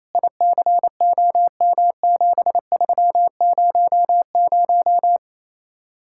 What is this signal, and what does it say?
Morse code, heard as a keyed tone, about 36 wpm, 700 Hz. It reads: ICOM7300